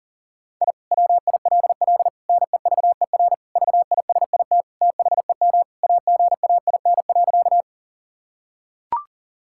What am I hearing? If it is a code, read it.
Morse code, heard as a keyed tone, about 37 wpm, 700 Hz. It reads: I WILL NEVER VISIT THEM AGAIN.
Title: Sentences from Top 1000 Words - Review Part 6 - 40wpm